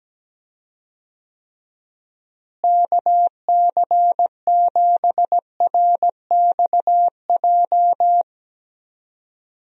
Morse code, heard as a keyed tone, about 17 wpm, 700 Hz. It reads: KC7RXJ
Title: USA Call Signs - 6 Characters - Repeated 3x - 17wpm